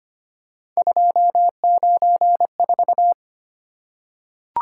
Morse code, heard as a keyed tone, about 25 wpm, 700 Hz. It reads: 294
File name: Three Letter-Number Combinations - 25wpm